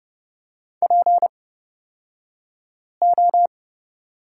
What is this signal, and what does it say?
Morse code, heard as a keyed tone, about 30 wpm, spaced as for 5 wpm, 700 Hz. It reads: PO
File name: Lesson 38d - Letter q - Two Characters 30wpm